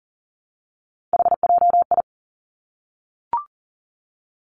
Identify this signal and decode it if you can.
Morse code, heard as a keyed tone, about 40 wpm, 700 Hz. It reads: HJI
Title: Three Challenging Letter-Number Combinations 40wpm